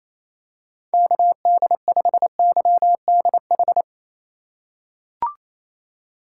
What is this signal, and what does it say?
Morse code, heard as a keyed tone, about 28 wpm, 700 Hz. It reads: KD5YDH
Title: USA Call Signs - 6 Characters - 3x Character Spacing - 28wpm